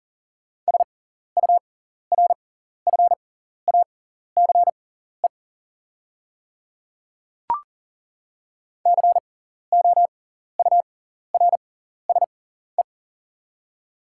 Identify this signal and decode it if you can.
Morse code, heard as a keyed tone, about 40 wpm, spaced as for 14 wpm, 700 Hz. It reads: SURFACE COURSE
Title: 6x Character Spacing - Top 500 Words - 40wpm